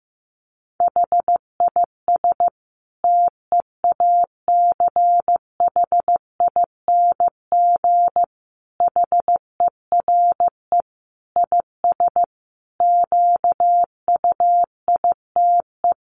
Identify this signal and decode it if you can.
Morse code, heard as a keyed tone, about 15 wpm, 700 Hz. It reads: HIS TEACHING HERE IS QUITE
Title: Sentences from Top 1100 Words 15wpm